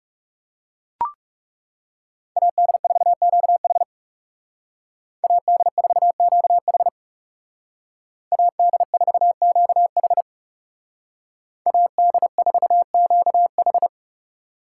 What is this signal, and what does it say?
Morse code, heard as a keyed tone, about 37 wpm, 700 Hz. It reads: AD4QH AD4QH AD4QH AD4QH
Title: Speed Racing - USA Call Signs 30wpm